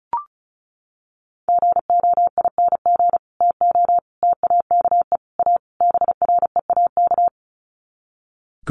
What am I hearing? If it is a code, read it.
Morse code, heard as a keyed tone, about 34 wpm, 700 Hz. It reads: GOING TO TAKE A BREAK
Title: Sets of 5 Words 35wpm